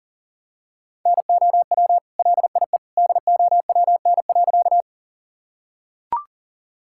Morse code, heard as a keyed tone, about 37 wpm, 700 Hz. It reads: NOW LIE DOWN.
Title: Sentences from Top 1600 Words 40wpm